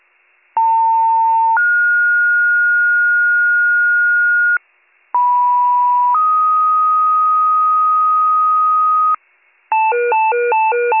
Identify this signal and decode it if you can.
Two-tone page, heard as a one-second tone, then a three-second tone, about 900 then 1430 Hz, about 980 then 1250 Hz, with noise. It is heard from 0.6 to 9.2 s.